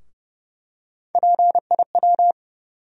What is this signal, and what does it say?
Morse code, heard as a keyed tone, about 29 wpm, 700 Hz. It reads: PIW
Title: USA Call Sign Postfixes - 3 Letters 30wpm